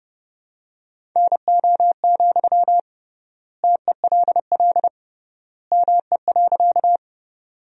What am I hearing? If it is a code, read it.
Morse code, heard as a keyed tone, about 30 wpm, 700 Hz. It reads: NO, TELL ME.
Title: 3x Word-Spacing - Sentences from Top 200 Words - 30wpm